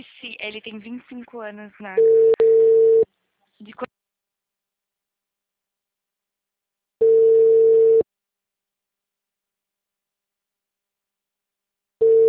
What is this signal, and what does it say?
Ringback tone (China, 450 Hz), with speech.